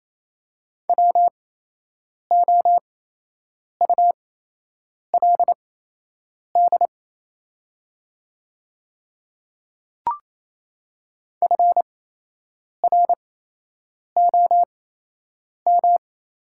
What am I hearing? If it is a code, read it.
Morse code, heard as a keyed tone, about 28 wpm, spaced as for 8 wpm, 700 Hz. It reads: WOULD FROM
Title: 8x Character Spacing - Top 100 Words - 28wpm